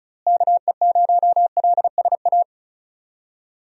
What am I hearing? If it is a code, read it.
Morse code, heard as a keyed tone, about 36 wpm, 700 Hz. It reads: KE0LSA
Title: USA Call Signs - 6 Characters - Repeated 2x - 35wpm